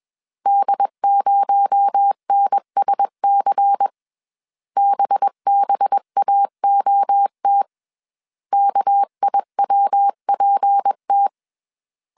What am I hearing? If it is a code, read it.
Morse code, heard as a keyed tone, about 21 wpm, 800 Hz. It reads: D0NSC BBAOT KIWPT
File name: CW / DET / 1 6 1500C S B 20wpm 2048